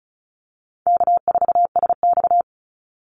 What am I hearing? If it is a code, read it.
Morse code, heard as a keyed tone, about 34 wpm, 700 Hz. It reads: K4SX
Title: Easy Call Signs Repeated 3x - 35wpm